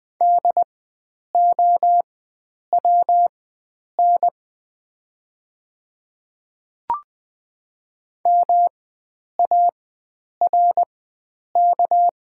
Morse code, heard as a keyed tone, about 20 wpm, spaced as for 9 wpm, 700 Hz. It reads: DOWN MARK
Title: 4x Character Spacing - Top 300 Words - 20wpm